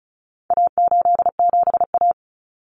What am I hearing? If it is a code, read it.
Morse code, heard as a keyed tone, about 34 wpm, 700 Hz. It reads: A87A
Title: Common Cut Numbers - 4-Digit 35wpm